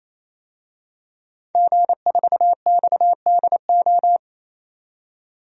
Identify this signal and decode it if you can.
Morse code, heard as a keyed tone, about 28 wpm, 700 Hz. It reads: G4XDO
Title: DXCC-100 Call Signs - 5 Characters - 28wpm